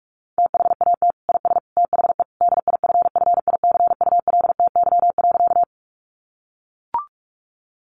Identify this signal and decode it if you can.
Morse code, heard as a keyed tone, about 44 wpm, 700 Hz. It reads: THAT IS THE DIFFICULTY.
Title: Sentences from Top 1500 Words 45wpm